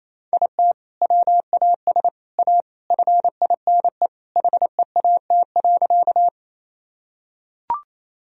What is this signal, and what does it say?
Morse code, heard as a keyed tone, about 28 wpm, 700 Hz. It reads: IT WAS A FINE HEAT.
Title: Sentences from Top 1000 Words - Review Part 3 - 28wpm